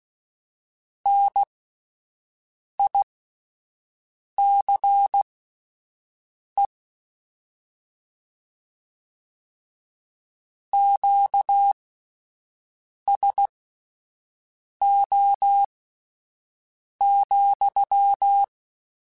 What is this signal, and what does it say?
Morse code, heard as a keyed tone, about 16 wpm, spaced as for 5 wpm, 800 Hz. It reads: NICE QSO,
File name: QSOs Spaced